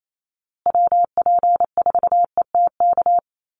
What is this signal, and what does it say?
Morse code, heard as a keyed tone, about 28 wpm, 700 Hz. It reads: WP4ETK
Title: USA Call Signs Repeated 2x 28wpm